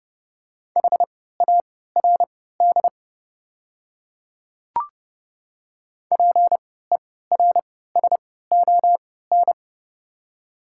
Morse code, heard as a keyed tone, about 32 wpm, spaced as for 17 wpm, 700 Hz. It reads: HARD PERSON
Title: 3x Character Spacing - Top 500 Words - 30wpm